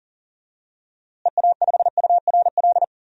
Morse code, heard as a keyed tone, about 40 wpm, 700 Hz. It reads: EA5URL
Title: DXCC-100 Call Signs - 6 Characters - Repeated 2x - 40wpm